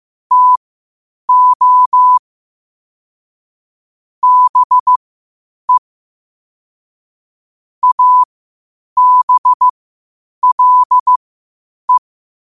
Morse code, heard as a keyed tone, about 15 wpm, spaced as for 8 wpm, 1000 Hz. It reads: TO BE ABLE
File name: qotdfiles / 8wpm